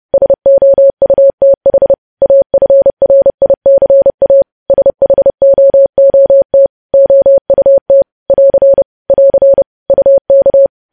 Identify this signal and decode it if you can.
Morse code, heard as a keyed tone, about 30 wpm, 550 Hz. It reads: SOUTH AFRICA SHOOT OUT <AR> <AR> UK
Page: news headlines in morse code at 30 wpm